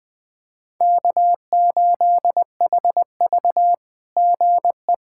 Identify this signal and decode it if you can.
Morse code, heard as a keyed tone, about 20 wpm, 700 Hz. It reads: K8HV GE
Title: SOTA QSO Practice 20wpm